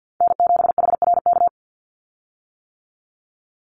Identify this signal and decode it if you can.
Morse code, heard as a keyed tone, about 48 wpm, 700 Hz. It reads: N7HRK
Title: Call Signs - 50wpm